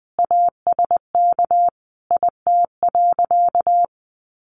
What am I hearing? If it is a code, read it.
Morse code, heard as a keyed tone, about 20 wpm, 700 Hz. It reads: ASK IT.
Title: Sentences from Top 1300 Words 20wpm